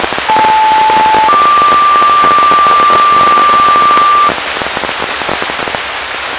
Two-tone page, heard as a one-second tone, then a three-second tone, about 870 then 1180 Hz, with noise. It is heard from 0.3 to 4.3 s.